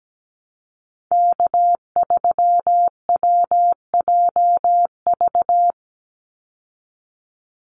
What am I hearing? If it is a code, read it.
Morse code, heard as a keyed tone, about 17 wpm, 700 Hz. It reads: K3WJV
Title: Call Signs Repeated 3x - 17wpm